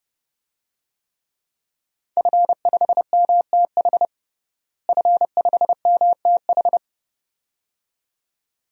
Morse code, heard as a keyed tone, about 32 wpm, 700 Hz. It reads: F5MTH F5MTH
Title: DXCC-100 Call Signs - 5 Characters - Repeated 2x - 30wpm